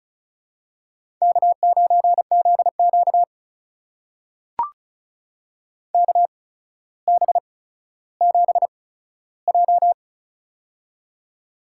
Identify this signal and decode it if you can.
Morse code, heard as a keyed tone, about 36 wpm, spaced as for 15 wpm, 700 Hz. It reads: K9ZQ KB7J
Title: USA Call Signs - 8x Character Spacing - 35wpm